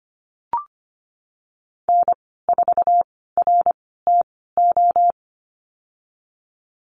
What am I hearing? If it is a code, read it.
Morse code, heard as a keyed tone, about 25 wpm, spaced as for 16 wpm, 700 Hz. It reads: N4RTO
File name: Lesson 15g - Number 5 - Call Signs 25wpm